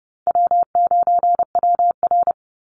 Morse code, heard as a keyed tone, about 30 wpm, 700 Hz. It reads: W9WR
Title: Easy Call Signs Repeated 2x - 30wpm